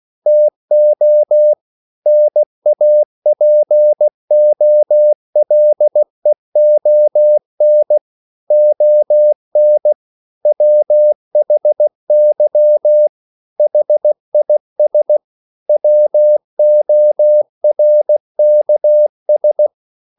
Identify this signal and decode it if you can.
Morse code, heard as a keyed tone, about 16 wpm, 600 Hz. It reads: TO NAPOLEON ON WHY HIS WORKS
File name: CW / Practice / EnglishQuotations0416